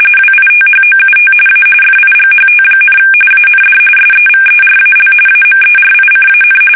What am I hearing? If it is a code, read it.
RTTY, about 45.45 baud. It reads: WITHOUT A FIGHT, AND SEVERAL AMERICAN
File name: fsk850 / usnavy2